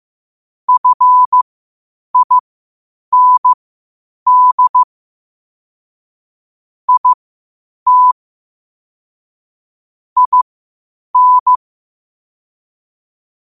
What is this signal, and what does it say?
Morse code, heard as a keyed tone, about 15 wpm, spaced as for 8 wpm, 1000 Hz. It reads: FIND IT IN
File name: qotdfiles / 8wpm